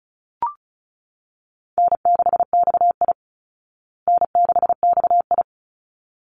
Morse code, heard as a keyed tone, about 34 wpm, 700 Hz. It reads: N6XI N6XI
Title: Easy Call Signs Repeated 2x - 35wpm